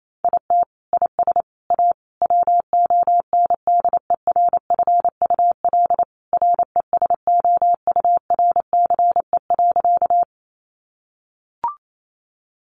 Morse code, heard as a keyed tone, about 28 wpm, 700 Hz. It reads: IT IS A WONDERFUL RESOURCE.
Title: Sentences from Top 1300 Words 28wpm